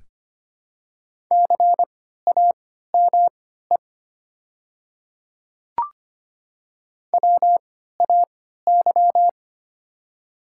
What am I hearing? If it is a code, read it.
Morse code, heard as a keyed tone, about 25 wpm, spaced as for 14 wpm, 700 Hz. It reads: CAME WAY
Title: 3x Character Spacing - Top 200 Words - 25wpm